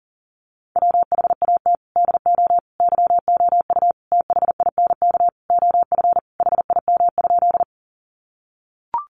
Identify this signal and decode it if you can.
Morse code, heard as a keyed tone, about 40 wpm, 700 Hz. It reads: WHAT DO YOU THINK OF HIM?
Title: Sentences from Top 200 Words 40wpm